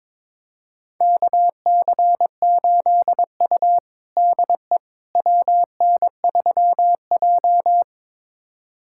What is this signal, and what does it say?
Morse code, heard as a keyed tone, about 22 wpm, 700 Hz. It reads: KC8U DE WN3J
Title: USA Callsign DE Callsign - 4 Characters - 22wpm